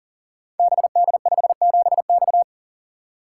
Morse code, heard as a keyed tone, about 39 wpm, 700 Hz. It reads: BD57X
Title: Code Groups - 5 Letter:Numbers - 40wpm